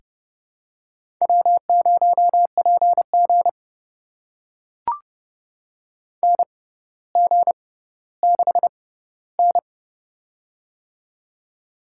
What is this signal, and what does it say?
Morse code, heard as a keyed tone, about 30 wpm, spaced as for 15 wpm, 700 Hz. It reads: W0PG NG6N
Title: USA Call Signs - 4 Characters - 6x Character Spacing - 30wpm